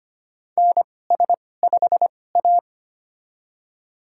Morse code, heard as a keyed tone, about 25 wpm, spaced as for 18 wpm, 700 Hz. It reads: NS5A
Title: USA Call Signs - 4 Characters - 2x Character Spacing - 25wpm